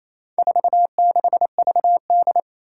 Morse code, heard as a keyed tone, about 27 wpm, 700 Hz. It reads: 46VD
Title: Four Challenging Letter-Number Combinations 28wpm